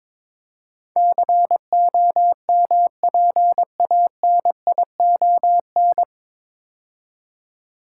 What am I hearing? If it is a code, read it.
Morse code, heard as a keyed tone, about 21 wpm, 700 Hz. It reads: COMPANION
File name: 9 Letter Words - First 3 Letters Given - 22wpm